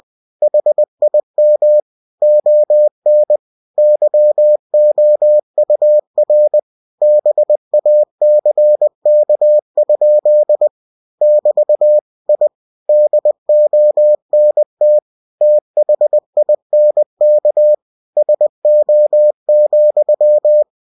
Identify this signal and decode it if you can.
Morse code, heard as a keyed tone, about 20 wpm, 600 Hz. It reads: HIM ON YOUR BACK? <BT> I DONT THINK SO,
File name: cw-books / winnie-the-pooh / winnie-20wpm-page-0098